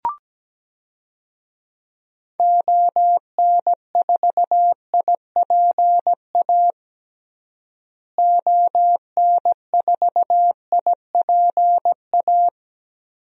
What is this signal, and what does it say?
Morse code, heard as a keyed tone, about 17 wpm, 700 Hz. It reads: ON4IPA ON4IPA
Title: DXCC-100 Call Signs - 6 Characters - Repeated 3x - 17wpm